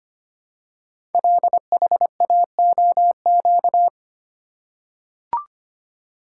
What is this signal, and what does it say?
Morse code, heard as a keyed tone, about 25 wpm, 700 Hz. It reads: LHAOQ
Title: Code Groups - 5 Letters - 25wpm